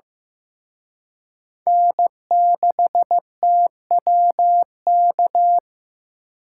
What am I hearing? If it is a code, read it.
Morse code, heard as a keyed tone, about 18 wpm, 700 Hz. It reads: N6TWK
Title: Speed Racing - USA 1x3 Call Signs 15wpm